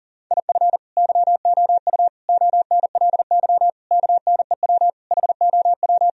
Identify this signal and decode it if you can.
Morse code, heard as a keyed tone, about 40 wpm, 700 Hz. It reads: IF YOU ONLY KNEW HOW
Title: Sentences from Top 1000 Words - Review Part 4 - 40wpm